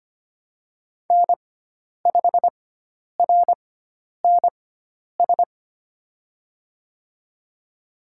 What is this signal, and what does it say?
Morse code, heard as a keyed tone, about 25 wpm, spaced as for 10 wpm, 700 Hz. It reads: N5RNS
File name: Lesson 16g - Letter c - Call Signs 25wpm